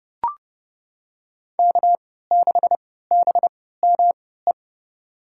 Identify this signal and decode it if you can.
Morse code, heard as a keyed tone, about 30 wpm, spaced as for 17 wpm, 700 Hz. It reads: K6BME
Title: USA Call Signs - 3x Character Spacing - 30wpm